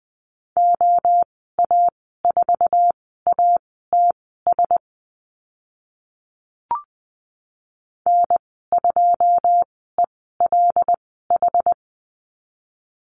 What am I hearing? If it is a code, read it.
Morse code, heard as a keyed tone, about 20 wpm, spaced as for 14 wpm, 700 Hz. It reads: OA4ATS N2ELH
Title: 082 Lesson 14g - Number 2 - Call Signs 20wpm